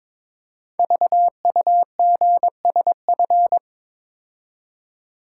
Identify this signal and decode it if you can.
Morse code, heard as a keyed tone, about 22 wpm, 700 Hz. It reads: VUGSF
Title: Code Groups - 5 Letters - 22wpm